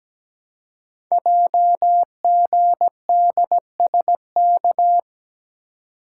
Morse code, heard as a keyed tone, about 17 wpm, 700 Hz. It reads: JGDSK
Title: Code Groups - 5 Letter:Numbers - 17wpm